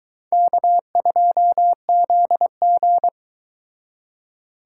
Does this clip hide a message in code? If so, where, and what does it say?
Yes: Morse code from 0.3 to 3.1 s, heard as a keyed tone, about 24 wpm, 700 Hz. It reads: K2ZG